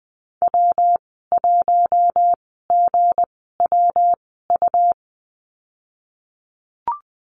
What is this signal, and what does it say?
Morse code, heard as a keyed tone, about 20 wpm, spaced as for 15 wpm, 700 Hz. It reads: W1GWU
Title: 173 Lesson 28g - Number 9 - Call Signs 20wpm